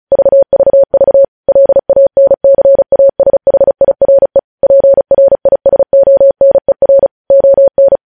Morse code, heard as a keyed tone, about 36 wpm, 550 Hz. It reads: VVV LANCASHIRE PRISONER ON
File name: podcast / News headlines-1x-35x35